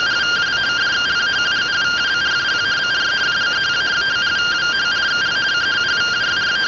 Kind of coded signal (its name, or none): two-tone teleprinter signal